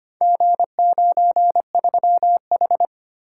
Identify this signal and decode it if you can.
Morse code, heard as a keyed tone, about 25 wpm, 700 Hz. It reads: G93H
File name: practice / 25wpm / Four Letter-Number Combinations - 25wpm